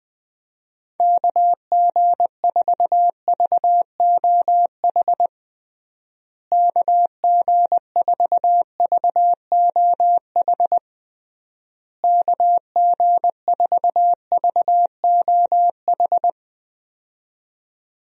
Morse code, heard as a keyed tone, about 20 wpm, 700 Hz. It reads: KG4VOH KG4VOH KG4VOH
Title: USA Call Signs - 6 Characters - Repeated 3x - 20wpm